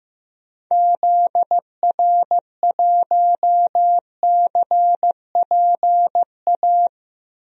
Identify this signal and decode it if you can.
Morse code, heard as a keyed tone, about 15 wpm, 700 Hz. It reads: ZR1CPA
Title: DXCC-100 Call Signs - 6 Characters - 15wpm